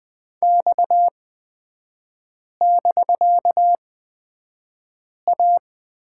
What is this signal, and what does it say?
Morse code, heard as a keyed tone, about 20 wpm, spaced as for 5 wpm, 700 Hz. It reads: X<BK>A
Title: 250 Lesson 40e - BK - Three Characters 20wpm